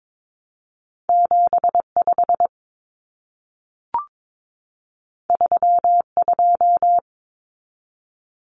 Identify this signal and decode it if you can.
Morse code, heard as a keyed tone, about 22 wpm, 700 Hz. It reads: 75 32
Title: Common Cut Numbers - 2-Digit 22wpm